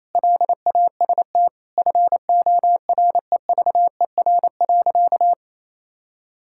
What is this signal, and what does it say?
Morse code, heard as a keyed tone, about 28 wpm, 700 Hz. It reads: LAST FOREVER.
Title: Sentences from Top 1600 Words 28wpm